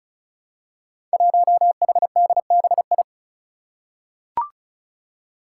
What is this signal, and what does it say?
Morse code, heard as a keyed tone, about 36 wpm, 700 Hz. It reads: 1HDBI
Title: Five Challenging Letter-Number Combinations 35wpm